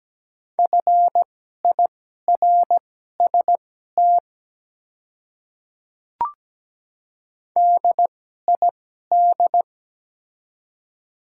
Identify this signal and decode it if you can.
Morse code, heard as a keyed tone, about 17 wpm, spaced as for 12 wpm, 700 Hz. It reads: FIRST DID
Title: 2x Character Spacing - Top 100 Words - 17wpm